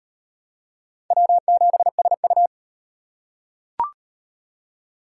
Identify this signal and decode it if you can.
Morse code, heard as a keyed tone, about 38 wpm, 700 Hz. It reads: W7SU